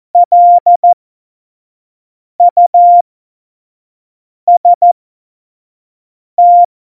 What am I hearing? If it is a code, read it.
Morse code, heard as a keyed tone, about 14 wpm, spaced as for 5 wpm, 700 Hz. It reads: LUST
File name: Morsekurs / Witze / Witz 07. November 2017, 25 F70